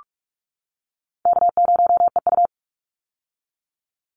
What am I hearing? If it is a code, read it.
Morse code, heard as a keyed tone, about 46 wpm, 700 Hz. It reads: K0EU
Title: Easy Call Signs - 45wpm